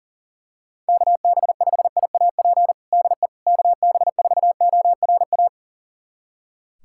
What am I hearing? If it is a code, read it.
Morse code, heard as a keyed tone, about 40 wpm, 690 Hz. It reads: KB5IAP DE KB4ORA
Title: USA Callsign DE Callsign - 6 Characters - 40wpm